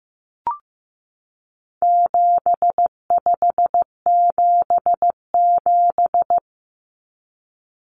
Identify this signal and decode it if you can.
Morse code, heard as a keyed tone, about 15 wpm, 700 Hz. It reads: 7577
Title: Common Cut Numbers - 4-Digit 15wpm